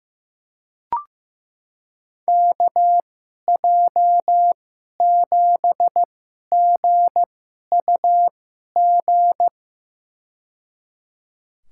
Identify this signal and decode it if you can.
Morse code, heard as a keyed tone, about 15 wpm, spaced as for 11 wpm, 700 Hz. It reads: KJ7GUG
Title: USA Call Signs - 2x Character Spacing - 15wpm